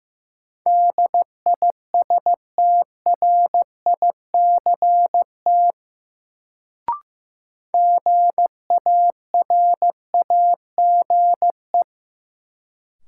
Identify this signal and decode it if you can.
Morse code, heard as a keyed tone, about 15 wpm, 700 Hz. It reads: DISTRICT GARAGE
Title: Sentences from Top 1600 Words - New Word Drill Alt - 15wpm